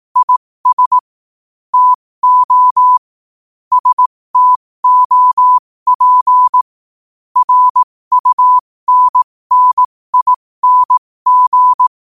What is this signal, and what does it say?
Morse code, heard as a keyed tone, about 18 wpm, spaced as for 15 wpm, 1000 Hz. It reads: IS TO STOP RUNNING